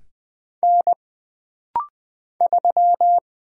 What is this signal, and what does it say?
Morse code, heard as a keyed tone, about 20 wpm, 700 Hz. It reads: N 3
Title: Single Character (Letter:Number:Punctuation:Prosign) - Rapid-Fire - 20wpm